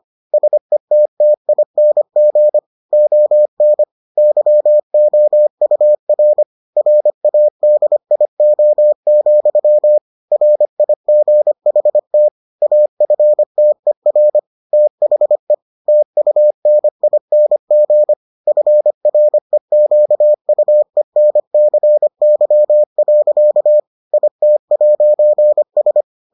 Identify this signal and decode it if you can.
Morse code, heard as a keyed tone, about 25 wpm, 600 Hz. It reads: SETTING ON YOUR RADIO, RIGHT AFTER THE TUNING FREQUENCY. IT'S